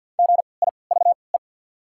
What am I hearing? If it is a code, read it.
Morse code, heard as a keyed tone, about 53 wpm, spaced as for 29 wpm, 690 Hz. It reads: GIVE
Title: 3x Character Spacing - Top 500 Words - 50wpm